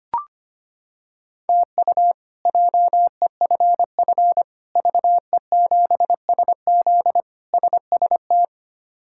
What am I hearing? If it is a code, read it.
Morse code, heard as a keyed tone, about 25 wpm, 700 Hz. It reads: TU JEFF VE7SZ SST
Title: K1USN SST QSO Practice - 25wpm